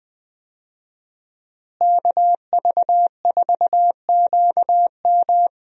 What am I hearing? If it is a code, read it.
Morse code, heard as a keyed tone, about 20 wpm, 700 Hz. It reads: KV4QM
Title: USA Call Signs - 5 Characters - Repeated 2x - 20wpm